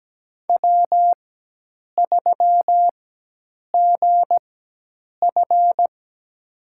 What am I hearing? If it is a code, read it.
Morse code, heard as a keyed tone, about 17 wpm, spaced as for 8 wpm, 700 Hz. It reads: W3GF
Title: USA Call Signs - 4 Characters - 4x Character Spacing - 17wpm